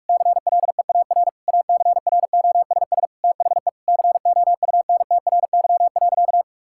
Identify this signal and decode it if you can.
Morse code, heard as a keyed tone, about 46 wpm, 700 Hz. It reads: CLEAR ACROSS THE COUNTRY.
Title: Sentences from Top 1000 Words - Review Part 5 - 45wpm